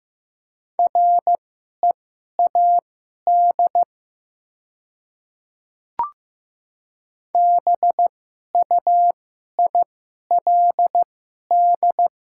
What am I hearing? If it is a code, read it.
Morse code, heard as a keyed tone, about 15 wpm, spaced as for 11 wpm, 700 Hz. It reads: READ BUILD
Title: 2x Character Spacing - Top 500 Words - 15wpm